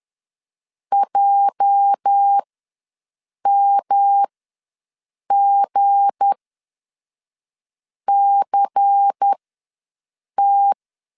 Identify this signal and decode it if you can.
Morse code, heard as a keyed tone, about 10 wpm, spaced as for 5 wpm, 800 Hz. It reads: JMGCT
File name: LEZIONI / 8 YJ45 / 8 YJ4 ws 10wpm 4020